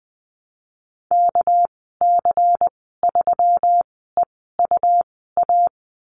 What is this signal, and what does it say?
Morse code, heard as a keyed tone, about 20 wpm, spaced as for 15 wpm, 700 Hz. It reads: KC3EUA
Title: 238 Lesson 38g - Letter q - Call Signs 20wpm